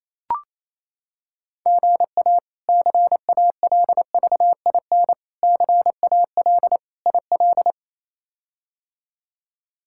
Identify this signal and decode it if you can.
Morse code, heard as a keyed tone, about 28 wpm, 700 Hz. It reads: GA CALVIN CAL IL
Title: K1USN SST QSO Practice - 28wpm